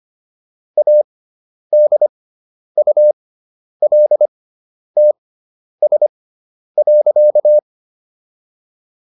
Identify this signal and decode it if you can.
Morse code, heard as a keyed tone, about 25 wpm, spaced as for 10 wpm, 600 Hz. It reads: ADULTS.